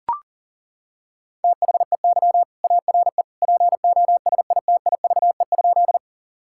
Morse code, heard as a keyed tone, about 40 wpm, 700 Hz. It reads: THEY ARE POSITIVE?
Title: Sentences from Top 1000 Words - Review Part 5 - 40wpm